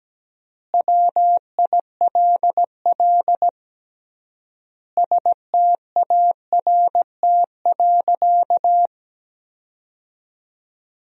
Morse code, heard as a keyed tone, about 17 wpm, 700 Hz. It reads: WILL START.
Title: 3x Word-Spacing - Sentences from Top 300 Words - 17wpm